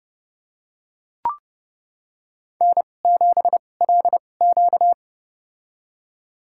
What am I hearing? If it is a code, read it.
Morse code, heard as a keyed tone, about 30 wpm, spaced as for 22 wpm, 700 Hz. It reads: N7LQ
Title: USA Call Signs - 4 Characters - 2x Character Spacing - 30wpm